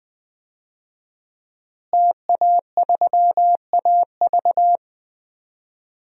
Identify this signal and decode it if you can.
Morse code, heard as a keyed tone, about 20 wpm, 700 Hz. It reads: TA3AV
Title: DXCC-100 Call Signs - 5 Characters - Repeated 2x - 20wpm